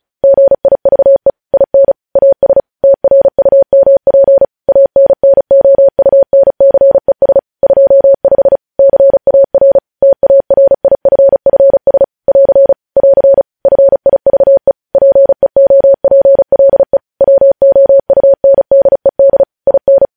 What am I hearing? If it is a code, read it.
Morse code, heard as a keyed tone, about 36 wpm, 550 Hz. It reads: GIVE IN AS TRUMP ANNOUNCES 25 CAR TARIFFS <AR> <AR> FIVE PEOPLE WOUNDED IN
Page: News headlines in Morse code at 35 wpm